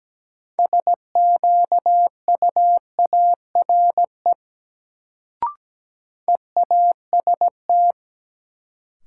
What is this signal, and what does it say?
Morse code, heard as a keyed tone, about 17 wpm, 700 Hz. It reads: SQUARE EAST
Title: Sentences from Top 1200 Words - New Word Drill Alt - 17wpm